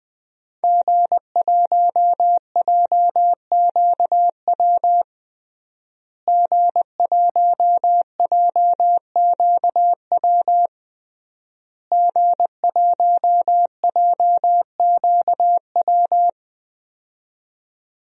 Morse code, heard as a keyed tone, about 20 wpm, 700 Hz. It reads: G1JQW G1JQW G1JQW
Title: DXCC-100 Call Signs - 5 Characters - Repeated 3x - 20wpm